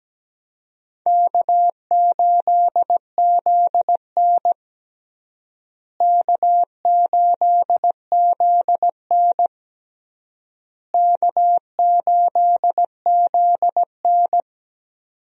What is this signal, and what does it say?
Morse code, heard as a keyed tone, about 17 wpm, 700 Hz. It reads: K8ZN K8ZN K8ZN
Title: USA Call Signs - 4 Characters - Repeated 3x - 17wpm